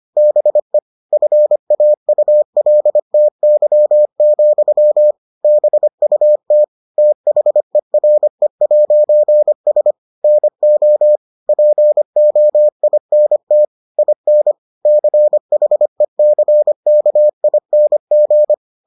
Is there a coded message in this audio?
Yes: Morse code from 0.2 to 18.5 s, heard as a keyed tone, about 25 wpm, 600 Hz. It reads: BE FAULTY, BUT THERE'S NO POINT IN CHECKING